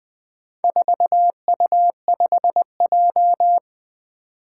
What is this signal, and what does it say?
Morse code, heard as a keyed tone, about 21 wpm, 700 Hz. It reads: 4U5J
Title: Four Challenging Letter-Number Combinations 20wpm